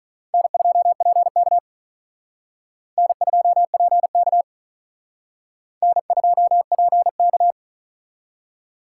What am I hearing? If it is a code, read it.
Morse code, heard as a keyed tone, about 42 wpm, 700 Hz. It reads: N2PK N2PK N2PK
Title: Speed Racing - USA 1x2 Call Signs 35wpm